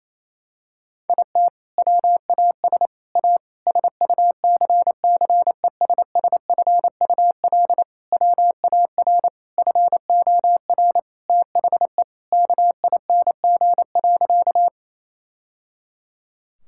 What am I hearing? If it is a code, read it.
Morse code, heard as a keyed tone, about 28 wpm, 700 Hz. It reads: IT WAS A SUCCESSFUL WAR FOR THE KING.
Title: Sentences from Top 1000 Words - Review Part 6 - 28wpm